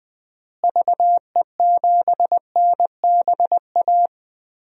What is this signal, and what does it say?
Morse code, heard as a keyed tone, about 20 wpm, 700 Hz. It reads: VE7NBA
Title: DXCC-100 Call Signs - 6 Characters - Repeated 2x - 20wpm